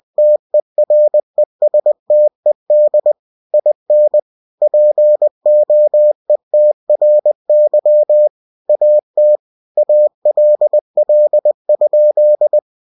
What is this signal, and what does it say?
Morse code, heard as a keyed tone, about 20 wpm, 600 Hz. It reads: TERESTED IN POETRY AT ALL?
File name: cw-books / winnie-the-pooh / winnie-20wpm-page-0066